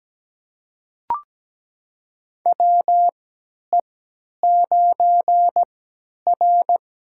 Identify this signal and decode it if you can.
Morse code, heard as a keyed tone, about 17 wpm, spaced as for 10 wpm, 700 Hz. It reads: WE9R
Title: USA Call Signs - 4 Characters - 3x Character Spacing - 17wpm